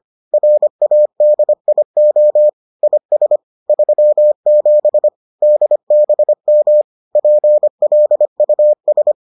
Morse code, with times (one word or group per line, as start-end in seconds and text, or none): 0.3-2.5 RADIO
2.8-3.4 IS
3.7-5.1 37
5.4-6.8 DBM
7.2-9.1 PLUS